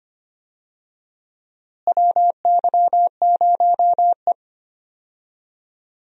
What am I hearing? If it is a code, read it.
Morse code, heard as a keyed tone, about 25 wpm, 700 Hz. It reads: WY0E
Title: USA Call Signs - 4 Characters - Repeated 3x - 25wpm